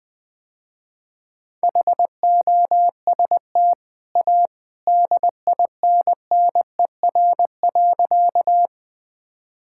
Morse code, heard as a keyed tone, about 20 wpm, 700 Hz. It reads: HOST A DINNER.
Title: Easy Sentences with Hints - 20wpm